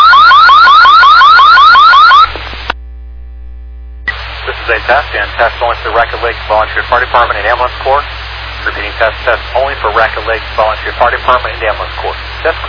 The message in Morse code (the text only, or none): none